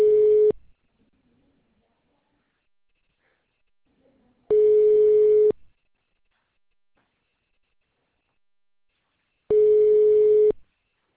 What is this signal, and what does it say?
Ringback tone (Europe, 425 Hz).